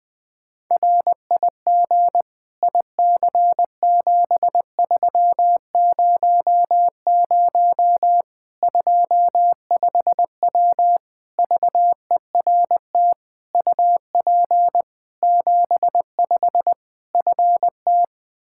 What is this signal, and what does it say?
Morse code, heard as a keyed tone, about 20 wpm, 700 Hz. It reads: RIG IC7300 25W VERT UP 75 FT
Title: Practice QSO - Exchange 2 - Info - 20wpm